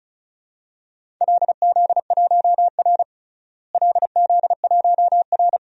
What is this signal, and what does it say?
Morse code, heard as a keyed tone, about 36 wpm, 700 Hz. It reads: LZ1R LZ1R
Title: DXCC-100 Call Signs - 4 Characters - Repeated 2x - 35wpm